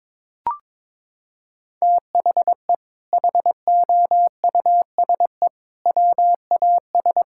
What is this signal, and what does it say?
Morse code, heard as a keyed tone, about 21 wpm, 700 Hz. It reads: THE HOUSE WAS
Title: Sentences from Top 1600 Words 22wpm